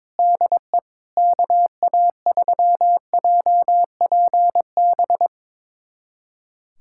Morse code, heard as a keyed tone, about 22 wpm, 700 Hz. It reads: DE KA3JPB
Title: USA Callsign DE Callsign - 6 Characters - 22wpm